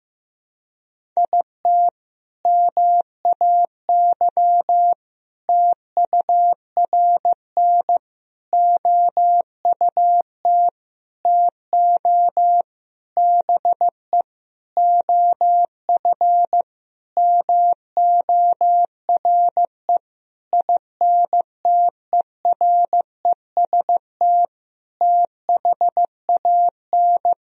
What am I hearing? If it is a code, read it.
Morse code, heard as a keyed tone, about 15 wpm, 700 Hz. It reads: IT MAY TURN OUT TO BE OF MORE INTEREST THAN
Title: Sentences from Top 400 Words and Abbreviations 15wpm